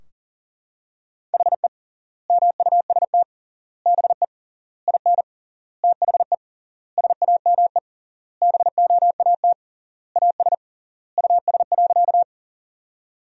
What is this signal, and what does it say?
Morse code, heard as a keyed tone, about 40 wpm, 700 Hz. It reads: HE MUST BE IN THE SAME BOAT AS US.
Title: 3x Word-Spacing - Sentences from Top 500 Words - 40wpm